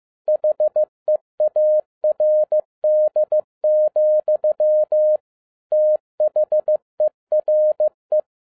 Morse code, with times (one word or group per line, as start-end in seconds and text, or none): 0.3-5.2 HEARD,
5.7-8.2 THERE